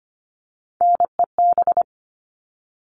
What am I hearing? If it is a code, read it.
Morse code, heard as a keyed tone, about 25 wpm, 700 Hz. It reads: NEB
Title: All Cut Numbers - 3-Digit 25wpm